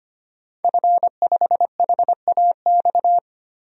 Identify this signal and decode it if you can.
Morse code, heard as a keyed tone, about 25 wpm, 700 Hz. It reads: F5HAX
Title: DXCC-100 Call Signs - 5 Characters - 25wpm